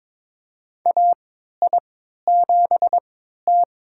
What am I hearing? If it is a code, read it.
Morse code, heard as a keyed tone, about 22 wpm, spaced as for 13 wpm, 700 Hz. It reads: AI7T
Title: USA Call Signs - 3x Character Spacing - 22wpm